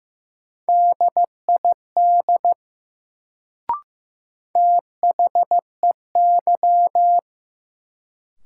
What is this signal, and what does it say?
Morse code, heard as a keyed tone, about 15 wpm, 700 Hz. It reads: DID THEY
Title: Sentences from Top 100 Words - New Word Drill - 15wpm